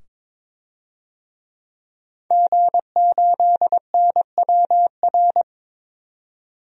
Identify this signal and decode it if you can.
Morse code, heard as a keyed tone, about 22 wpm, 700 Hz. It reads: G8NWR
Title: DXCC-100 Call Signs - 5 Characters - 22wpm